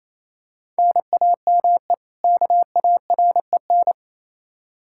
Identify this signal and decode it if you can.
Morse code, heard as a keyed tone, about 28 wpm, 700 Hz. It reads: NAME KAREN
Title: Practice QSO - Exchange 1 - Beginning - 28wpm